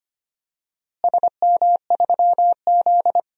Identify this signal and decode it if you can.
Morse code, heard as a keyed tone, about 25 wpm, 700 Hz. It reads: SM3Z
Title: DXCC-100 Call Signs - 4 Characters - Repeated 2x - 25wpm